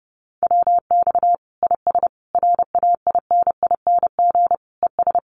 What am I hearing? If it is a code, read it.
Morse code, heard as a keyed tone, about 30 wpm, 700 Hz. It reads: WX IS RAINING ES
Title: Common QSO Phrases 30wpm